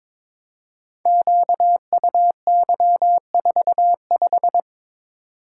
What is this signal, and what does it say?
Morse code, heard as a keyed tone, about 22 wpm, 700 Hz. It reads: QUY45
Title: Code Groups - 5 Letter:Numbers - 22wpm